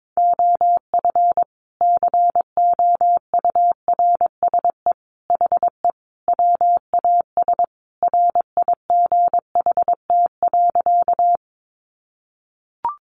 Morse code, heard as a keyed tone, about 22 wpm, 700 Hz. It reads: OF COURSE HE WAS RIGHT.
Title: Sentences from Top 500 Words 22wpm